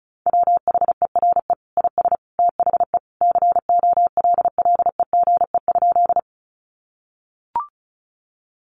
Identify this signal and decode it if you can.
Morse code, heard as a keyed tone, about 34 wpm, 700 Hz. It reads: WHERE IS THE COLLEGE?
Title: Sentences from Top 1300 Words 35wpm